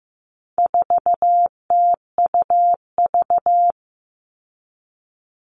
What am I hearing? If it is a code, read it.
Morse code, heard as a keyed tone, about 15 wpm, 700 Hz. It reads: 4TUV
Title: All Cut Numbers - 4-Digit 15wpm